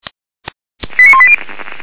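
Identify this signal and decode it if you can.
Five-tone selective call (ZVEI), heard as quick runs of five tones, tones of about 70 ms. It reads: 89180
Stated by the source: ZVEI five-tone call 89135 received by SDR.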